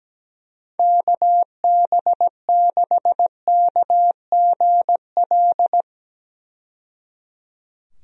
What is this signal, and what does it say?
Morse code, heard as a keyed tone, about 17 wpm, 700 Hz. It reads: KB6KGL
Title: USA Call Signs - 6 Characters - 17wpm